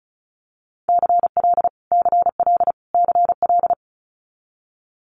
Morse code, heard as a keyed tone, about 36 wpm, 700 Hz. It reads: CL CL CL
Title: Multi-letter Phonemes Repeated 3x 35wpm